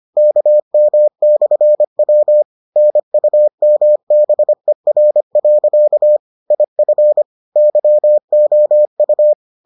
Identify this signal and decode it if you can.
Morse code, heard as a keyed tone, about 25 wpm, 600 Hz. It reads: KM/W NUMBER. IF YOU